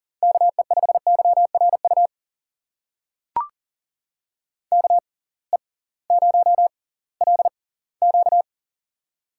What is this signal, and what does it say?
Morse code, heard as a keyed tone, about 40 wpm, spaced as for 22 wpm, 700 Hz. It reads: KE5YRU KE0LQ
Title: USA Call Signs - 6x Character Spacing - 40wpm